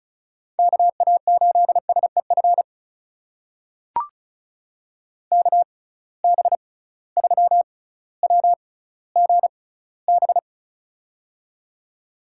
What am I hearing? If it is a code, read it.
Morse code, heard as a keyed tone, about 36 wpm, spaced as for 18 wpm, 700 Hz. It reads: KA8SEF KB3WGB
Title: USA Call Signs - 6 Characters - 6x Character Spacing - 35wpm